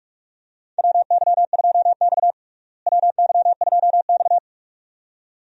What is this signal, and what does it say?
Morse code, heard as a keyed tone, about 43 wpm, 700 Hz. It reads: WY2X WY2X
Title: USA Call Signs - 4 Characters - Repeated 2x - 45wpm